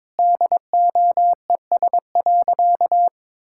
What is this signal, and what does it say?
Morse code, heard as a keyed tone, about 22 wpm, 700 Hz. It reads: DOES.
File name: practice / 22wpm / Sentences with 2-Word Phrases - 22wpm